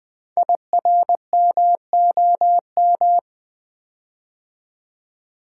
Morse code, heard as a keyed tone, about 20 wpm, 700 Hz. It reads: IRMOM
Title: Code Groups - 5 Letters - 20wpm